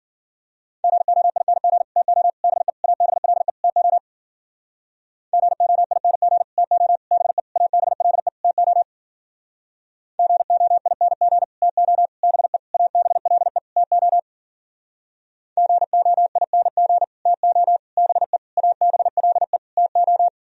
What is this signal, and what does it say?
Morse code, heard as a keyed tone, about 52 wpm, 700 Hz. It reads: GOING TO BE ABLE TO GOING TO BE ABLE TO GOING TO BE ABLE TO GOING TO BE ABLE TO